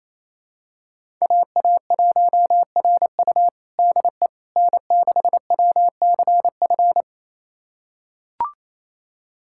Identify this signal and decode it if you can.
Morse code, heard as a keyed tone, about 28 wpm, 700 Hz. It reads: AA1RU DE N6WCF
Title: USA Callsign DE Callsign - 5 Characters - 28wpm